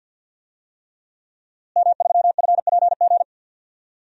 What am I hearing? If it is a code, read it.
Morse code, heard as a keyed tone, about 52 wpm, 700 Hz. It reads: M3FPG
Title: DXCC-100 Call Signs - 5 Characters - Repeated 3x - 50wpm